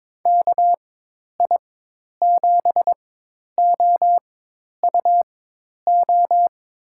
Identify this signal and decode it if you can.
Morse code, heard as a keyed tone, about 22 wpm, spaced as for 10 wpm, 700 Hz. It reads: KI7OUO
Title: USA Call Signs - 6 Characters - 4x Character Spacing - 22wpm